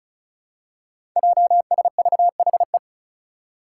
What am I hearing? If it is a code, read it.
Morse code, heard as a keyed tone, about 36 wpm, 700 Hz. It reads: JSVHE
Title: Five Challenging Letter-Number Combinations 35wpm